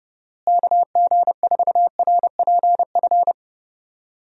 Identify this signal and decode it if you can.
Morse code, heard as a keyed tone, about 30 wpm, 700 Hz. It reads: KG4RPF
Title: USA Call Signs - 6 Characters - Repeated 3x - 30wpm